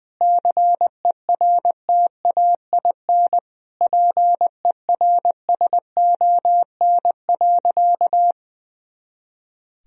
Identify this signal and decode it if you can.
Morse code, heard as a keyed tone, about 20 wpm, 700 Hz. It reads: CERTAIN PERSON.
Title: Sentences from Top 500 Words and Abbreviations 20wpm